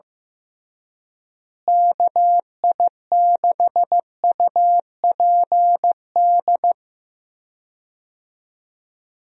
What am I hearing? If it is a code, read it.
Morse code, heard as a keyed tone, about 15 wpm, 700 Hz. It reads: KI6UPD
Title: USA Call Signs - 6 Characters - Repeated 3x - 15wpm